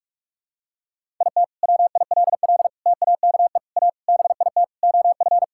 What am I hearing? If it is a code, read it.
Morse code, heard as a keyed tone, about 41 wpm, 700 Hz. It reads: IT WILL TAKE A BIT OF
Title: Sentences from Top 1000 Words - Review Part 5 - 45wpm